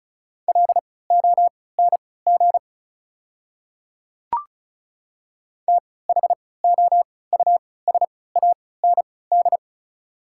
Morse code, heard as a keyed tone, about 34 wpm, spaced as for 20 wpm, 690 Hz. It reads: LONG THOUSAND
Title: 3x Character Spacing - Top 500 Words - 35wpm